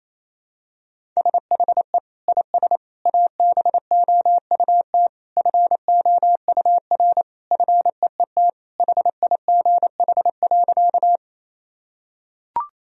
Morse code, heard as a keyed tone, about 28 wpm, 700 Hz. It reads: SHE IS ABOUT FOUR FEET HIGH.
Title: Sentences from Top 1000 Words - Review Part 1 - 28wpm